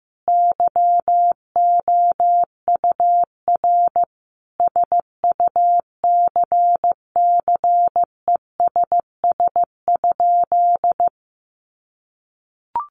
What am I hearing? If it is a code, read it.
Morse code, heard as a keyed tone, about 15 wpm, 700 Hz. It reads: YOUR SUCCESS?
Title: Sentences from Top 900 Words 15wpm